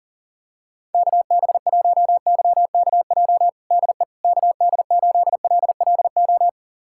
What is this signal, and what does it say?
Morse code, heard as a keyed tone, about 40 wpm, 700 Hz. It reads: KB1YKJ DE KD8LLO
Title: USA Callsign DE Callsign - 6 Characters - 40wpm